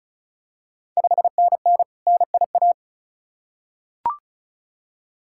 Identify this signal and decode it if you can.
Morse code, heard as a keyed tone, about 36 wpm, 700 Hz. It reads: 5NN NIA
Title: Ontario QSO Party Exchange 35wpm